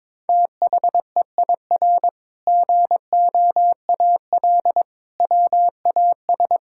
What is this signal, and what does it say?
Morse code, heard as a keyed tone, about 22 wpm, 700 Hz. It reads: THEIR GOAL WAS
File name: practice / 22wpm / Sentences with 8-Word Phrases - 22wpm